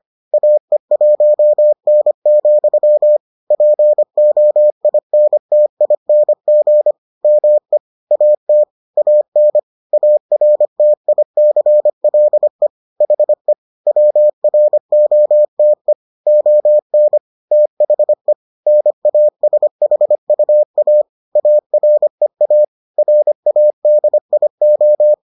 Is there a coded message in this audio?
Yes: Morse code from 0.3 to 25.1 s, heard as a keyed tone, about 25 wpm, 600 Hz. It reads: AE1N, POINTING ME AT AN ARTICLE HE WROTE ON THE NASHUA AREA RADIO